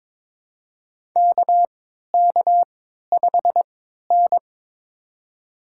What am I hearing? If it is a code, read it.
Morse code, heard as a keyed tone, about 22 wpm, spaced as for 12 wpm, 700 Hz. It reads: KK5N
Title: USA Call Signs - 3x Character Spacing - 22wpm